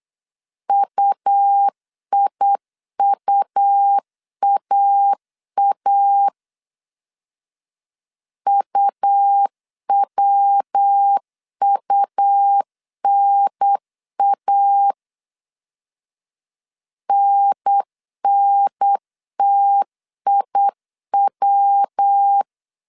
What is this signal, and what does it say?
Morse code, heard as a keyed tone, about 8 wpm, 800 Hz. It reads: UIUAA UWUNA NNTIW
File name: CW / DET / 1 3 1500C S B 8wpm 4141